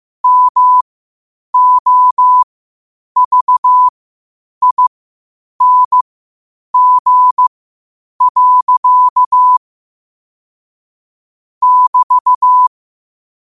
Morse code, heard as a keyed tone, about 15 wpm, spaced as for 8 wpm, 1000 Hz. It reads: MOVING. <BT>
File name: qotdfiles / 8wpm